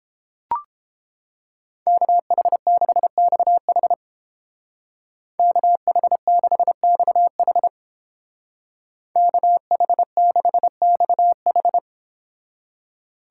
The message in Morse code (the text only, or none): KH6XH KH6XH KH6XH